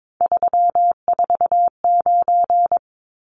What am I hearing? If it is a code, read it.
Morse code, heard as a keyed tone, about 22 wpm, 700 Hz. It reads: 349
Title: Numbers - 3-Digit 22wpm